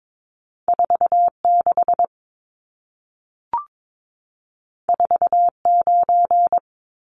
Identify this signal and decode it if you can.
Morse code, heard as a keyed tone, about 22 wpm, 700 Hz. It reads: 46 49
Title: Numbers - 2-Digit 22wpm